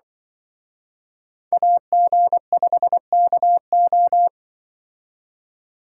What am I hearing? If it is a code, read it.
Morse code, heard as a keyed tone, about 25 wpm, 700 Hz. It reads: AG5KO
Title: Speed Racing - USA 2x2 Call Signs 20wpm